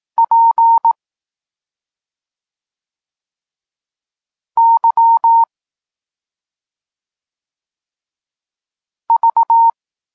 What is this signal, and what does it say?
Morse code, heard as a keyed tone, about 18 wpm, 920 Hz. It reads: P Y V
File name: Tekst-34 4